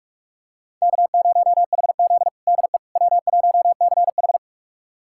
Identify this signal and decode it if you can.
Morse code, heard as a keyed tone, about 46 wpm, 700 Hz. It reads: K0HZ DE W1CH
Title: USA Callsign DE Callsign - 4 Characters - 45wpm